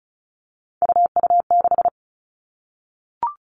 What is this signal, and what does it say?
Morse code, heard as a keyed tone, about 34 wpm, 700 Hz. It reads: UU6
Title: All Cut Numbers - 3-Digit 35wpm